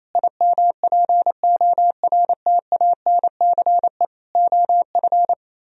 Morse code, heard as a keyed tone, about 34 wpm, 700 Hz. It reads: IMPORTANCE OF
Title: Speed-Racing - Sets of 3 Words 28wpm